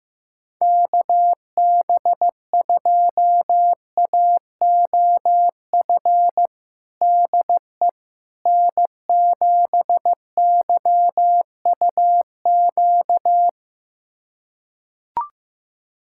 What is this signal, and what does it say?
Morse code, heard as a keyed tone, about 15 wpm, 700 Hz. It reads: KB2AOF DE N7YUQ
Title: USA Callsign DE Callsign 15wpm